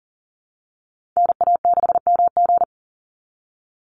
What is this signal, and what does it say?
Morse code, heard as a keyed tone, about 40 wpm, 700 Hz. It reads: NA6MG
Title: NA SOTA Chaser Callsigns 40wpm